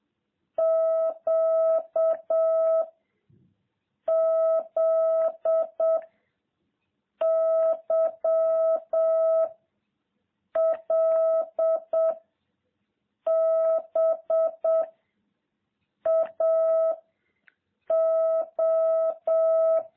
Morse code, heard as a keyed tone, about 7 wpm, 660 Hz. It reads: Q Z Y L B A O